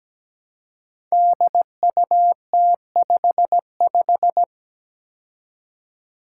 Morse code, heard as a keyed tone, about 17 wpm, 700 Hz. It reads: DUT55
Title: Five Challenging Letter-Number Combinations 17wpm